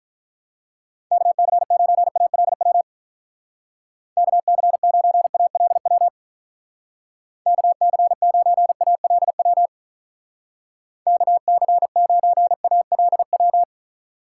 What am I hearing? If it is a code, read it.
Morse code, heard as a keyed tone, about 44 wpm, 700 Hz. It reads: KC9ALW KC9ALW KC9ALW KC9ALW